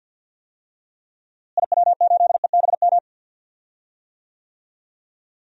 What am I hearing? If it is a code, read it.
Morse code, heard as a keyed tone, about 52 wpm, 700 Hz. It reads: IW8EBM
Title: DXCC-100 Call Signs - 6 Characters - Repeated 3x - 50wpm